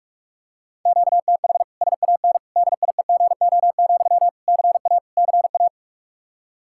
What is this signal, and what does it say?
Morse code, heard as a keyed tone, about 41 wpm, 700 Hz. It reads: QTH SAN DIEGO, CA CA
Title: Practice QSO - Exchange 1 - Beginning - 45wpm